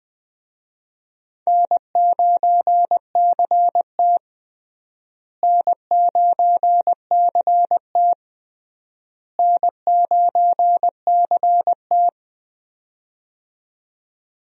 Morse code, heard as a keyed tone, about 20 wpm, 700 Hz. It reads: N9CT N9CT N9CT
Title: USA Call Signs - 4 Characters - Repeated 3x - 20wpm